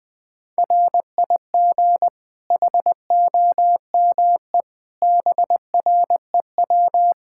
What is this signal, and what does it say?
Morse code, heard as a keyed tone, about 21 wpm, 700 Hz. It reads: RIG HOME BREW
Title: Practice QSO - Exchange 2 - Info - 20wpm